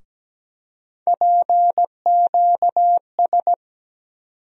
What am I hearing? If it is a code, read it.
Morse code, heard as a keyed tone, about 17 wpm, 700 Hz. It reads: PQS
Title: USA Call Sign Postfixes - 3 Letters 17wpm